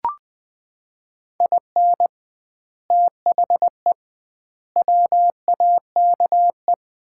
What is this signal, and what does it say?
Morse code, heard as a keyed tone, about 20 wpm, 700 Hz. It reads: IN THE WAKE
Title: 2x Word-Spacing - Sets of 5 Words - 20wpm